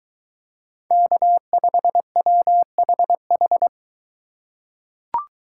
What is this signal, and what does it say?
Morse code, heard as a keyed tone, about 21 wpm, 700 Hz. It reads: K5WHH